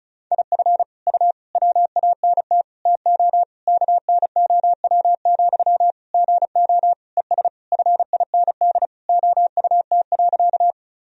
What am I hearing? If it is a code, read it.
Morse code, heard as a keyed tone, about 36 wpm, 700 Hz. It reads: IF U WANT TO KNOW, GO ES FIND OUT.
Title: Sentences from Top 200 Words and Abbreviations 35wpm